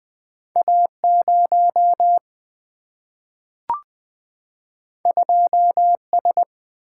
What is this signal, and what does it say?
Morse code, heard as a keyed tone, about 20 wpm, 700 Hz. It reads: A 0 2 S
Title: Single-Letter Single-Number 20wpm